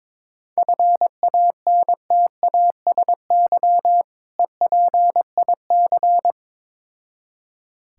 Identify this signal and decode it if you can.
Morse code, heard as a keyed tone, about 22 wpm, 690 Hz. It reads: FANTASY EPIC
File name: Difficult Words with Hints - 22wpm